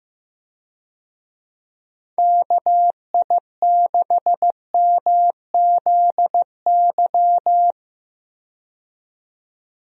Morse code, heard as a keyed tone, about 15 wpm, 700 Hz. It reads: KI6MZY
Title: USA Call Signs - 6 Characters - 15wpm